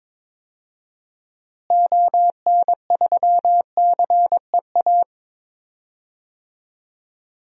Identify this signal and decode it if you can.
Morse code, heard as a keyed tone, about 22 wpm, 700 Hz. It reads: ON3CEA
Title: DXCC-100 Call Signs - 6 Characters - Repeated 3x - 22wpm